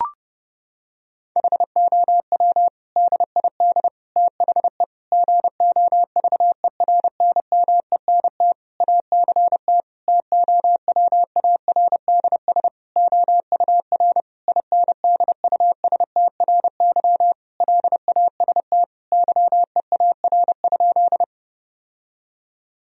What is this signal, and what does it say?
Morse code, heard as a keyed tone, about 30 wpm, 700 Hz. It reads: HOW DID THE GOVERNMENT ACT TOWARDS OUR INDUSTRY LAST YEAR?
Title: Sentences from Top 1000 Words - Review Part 4 - 30wpm